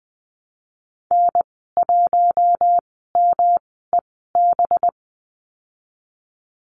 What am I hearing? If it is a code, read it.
Morse code, heard as a keyed tone, about 19 wpm, spaced as for 14 wpm, 700 Hz. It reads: N1MEB
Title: 200 Lesson 32g - Letter k - Call Signs 20wpm